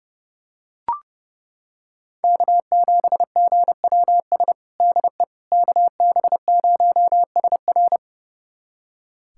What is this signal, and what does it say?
Morse code, heard as a keyed tone, about 30 wpm, 700 Hz. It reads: K7GWS DE KB0SR
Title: USA Callsign DE Callsign - 5 Characters - 30wpm